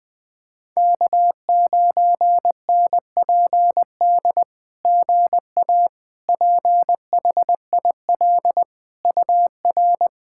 Morse code, heard as a keyed tone, about 20 wpm, 700 Hz. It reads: K9NPD GA PHIL UR
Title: SOTA QSO Practice 20wpm